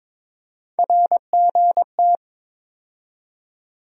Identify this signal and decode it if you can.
Morse code, heard as a keyed tone, about 22 wpm, 700 Hz. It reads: RGT
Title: USA Call Sign Postfixes - 3 Letters 22wpm